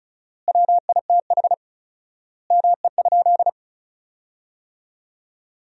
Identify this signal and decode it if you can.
Morse code, heard as a keyed tone, about 36 wpm, 700 Hz. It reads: WITH ME?
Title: 4x Word-Spacing - Sentences from Top 200 Words - 35wpm